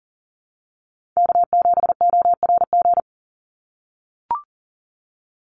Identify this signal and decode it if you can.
Morse code, heard as a keyed tone, about 40 wpm, 700 Hz. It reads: K7ORG
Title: NA SOTA Chaser Callsigns 40wpm